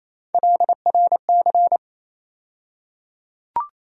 Morse code, heard as a keyed tone, about 27 wpm, 700 Hz. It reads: LRC
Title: USA Call Sign Postfixes - 3 Letters 28wpm